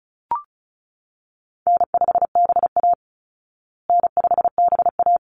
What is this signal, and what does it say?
Morse code, heard as a keyed tone, about 34 wpm, 700 Hz. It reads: N5BA N5BA
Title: Call Signs Repeated 3x - 35wpm